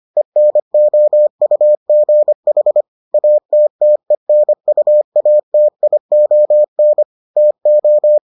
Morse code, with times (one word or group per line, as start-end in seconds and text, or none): 0.2-2.8 ENOUGH
3.1-7.0 ATTENUATION
7.4-8.2 TO